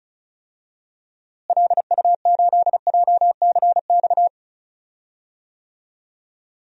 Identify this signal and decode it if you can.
Morse code, heard as a keyed tone, about 36 wpm, 700 Hz. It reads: LU8JCX
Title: DXCC-100 Call Signs - 6 Characters - Repeated 3x - 35wpm